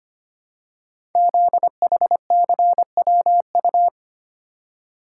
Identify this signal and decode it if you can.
Morse code, heard as a keyed tone, about 25 wpm, 700 Hz. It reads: ZHCWU
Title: Code Groups - 5 Letters - 25wpm